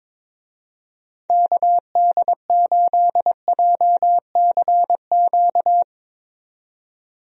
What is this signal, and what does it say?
Morse code, heard as a keyed tone, about 22 wpm, 700 Hz. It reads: KD8JCQ
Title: USA Call Signs - 6 Characters - 22wpm